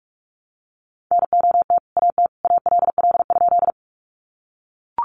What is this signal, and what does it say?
Morse code, heard as a keyed tone, about 44 wpm, 700 Hz. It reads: NOT AT ALL?
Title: Sentences from Top 600 Words 45wpm